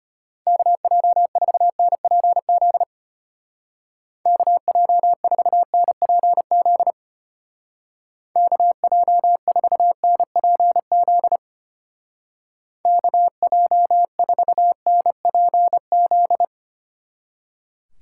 Morse code, heard as a keyed tone, about 33 wpm, 700 Hz. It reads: KJ4NPZ KJ4NPZ KJ4NPZ KJ4NPZ